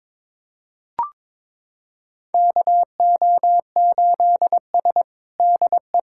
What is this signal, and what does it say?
Morse code, heard as a keyed tone, about 22 wpm, 700 Hz. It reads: KO8S DE
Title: USA Callsign DE Callsign - 4 Characters - 22wpm